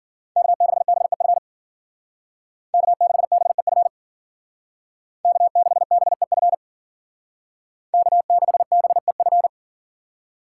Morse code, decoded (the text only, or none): K6BEF K6BEF K6BEF K6BEF